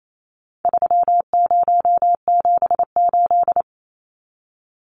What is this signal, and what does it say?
Morse code, heard as a keyed tone, about 28 wpm, 700 Hz. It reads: 3078
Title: Numbers - 4-Digit 28wpm